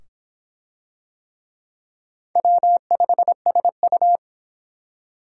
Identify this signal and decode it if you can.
Morse code, heard as a keyed tone, about 25 wpm, 700 Hz. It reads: W5SU